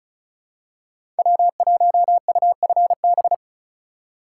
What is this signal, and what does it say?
Morse code, heard as a keyed tone, about 36 wpm, 700 Hz. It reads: W1UFB
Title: USA Call Signs - 5 Characters - 3x Character Spacing - 35wpm